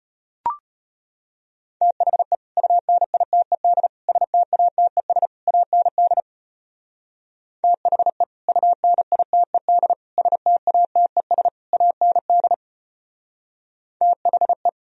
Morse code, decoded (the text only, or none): THE UNITED STATES AND THE UNITED STATES AND THE